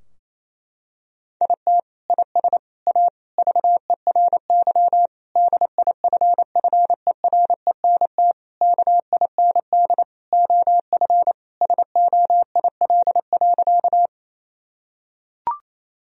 Morse code, heard as a keyed tone, about 27 wpm, 700 Hz. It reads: IT IS A VERY DIFFERENT KIND OF SOIL.
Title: Sentences from Top 1000 Words - Review Part 4 - 28wpm